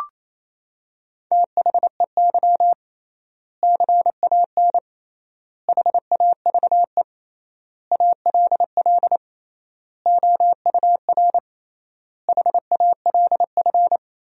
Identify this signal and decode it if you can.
Morse code, heard as a keyed tone, about 28 wpm, 700 Hz. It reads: THEY CAN HAVE ALL OUR HALF
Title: 3x Word-Spacing - Sentences from Top 400 Words - 28wpm